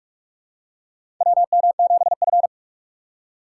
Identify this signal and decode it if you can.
Morse code, heard as a keyed tone, about 46 wpm, 700 Hz. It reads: WM7F
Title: USA Call Signs - 4 Characters - 4x Character Spacing - 45wpm